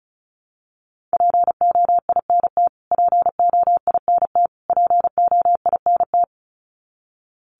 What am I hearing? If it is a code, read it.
Morse code, heard as a keyed tone, about 34 wpm, 700 Hz. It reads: POINT POINT POINT
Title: Top 500 Words Repeated 35wpm